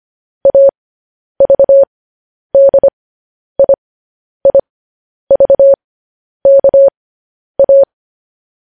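Morse code, heard as a keyed tone, about 25 wpm, spaced as for 10 wpm, 550 Hz. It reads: AVDIIVKA
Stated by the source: Morse transmission of news headlines.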